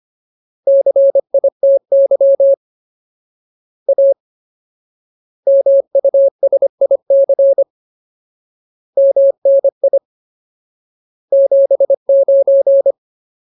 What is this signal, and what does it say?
Morse code, heard as a keyed tone, about 25 wpm, 550 Hz. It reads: CITY A MUSIC MNI 79
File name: Ultimate Word List / 25WPM Ultimate